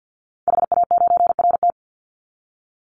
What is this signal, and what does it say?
Morse code, heard as a keyed tone, about 50 wpm, 700 Hz. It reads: HA9RT
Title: Call Signs - 50wpm